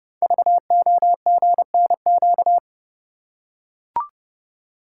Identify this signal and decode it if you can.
Morse code, heard as a keyed tone, about 30 wpm, 690 Hz. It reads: VOGNQ
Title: Code Groups - 5 Letters - 30wpm